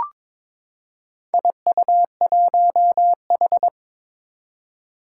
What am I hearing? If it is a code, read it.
Morse code, heard as a keyed tone, about 22 wpm, 700 Hz. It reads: IU1H
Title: Four Challenging Letter-Number Combinations 22wpm